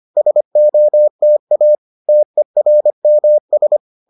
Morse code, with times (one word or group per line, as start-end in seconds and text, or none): 0.2-1.8 SOTA
2.1-3.8 TERMS